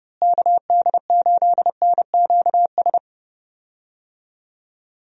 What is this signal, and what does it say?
Morse code, heard as a keyed tone, about 30 wpm, 700 Hz. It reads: KD8NQS
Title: USA Call Signs - 6 Characters - Repeated 3x - 30wpm